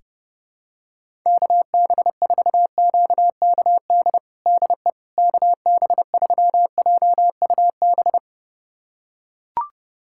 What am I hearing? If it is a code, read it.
Morse code, heard as a keyed tone, about 30 wpm, 700 Hz. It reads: KB4QKD DE KB3JUB
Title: USA Callsign DE Callsign - 6 Characters - 30wpm